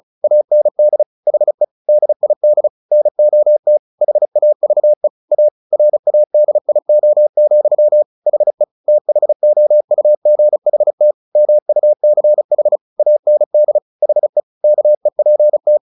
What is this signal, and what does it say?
Morse code, heard as a keyed tone, about 37 wpm, 600 Hz. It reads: AND HE DID NOT HAVE A RADIO, HE THOUGHT MUCH AND HE KEPT
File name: the old man and the sea 2 35wpm